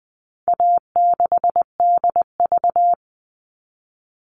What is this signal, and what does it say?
Morse code, heard as a keyed tone, about 20 wpm, 700 Hz. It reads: A6DV
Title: All Cut Numbers - 4-Digit 20wpm